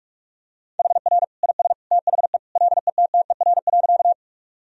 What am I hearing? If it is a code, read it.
Morse code, heard as a keyed tone, about 46 wpm, 700 Hz. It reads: HR IS THE LETTER.
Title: Sentences from Top 300 Words and Abbreviations 45wpm